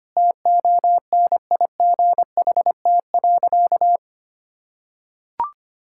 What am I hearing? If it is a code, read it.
Morse code, heard as a keyed tone, about 25 wpm, 700 Hz. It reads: TONIGHT.
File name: practice / 25wpm / Sentences with 3-Word Phrases - 25wpm